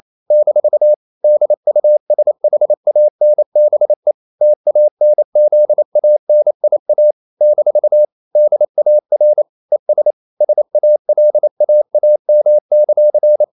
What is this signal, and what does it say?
Morse code, heard as a keyed tone, about 28 wpm, 600 Hz. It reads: - DUSHANBE TANZANIA - DAR ES SALAAM;
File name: CW / Practice / NationalCapitals28